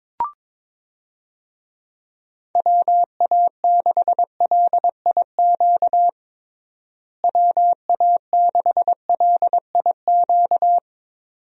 Morse code, heard as a keyed tone, about 21 wpm, 700 Hz. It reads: WA6LIQ WA6LIQ
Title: USA Call Signs - 6 Characters - Repeated 2x - 22wpm